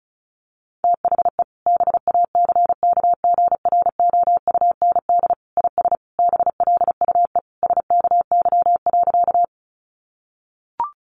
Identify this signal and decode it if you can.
Morse code, heard as a keyed tone, about 34 wpm, 700 Hz. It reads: THE BACKGROUND IS BLUE SKY.
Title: Sentences from Top 1300 Words 35wpm